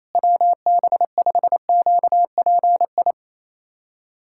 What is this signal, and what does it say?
Morse code, heard as a keyed tone, about 28 wpm, 700 Hz. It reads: WB5QPI
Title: USA Callsign DE Callsign - 6 Characters - 28wpm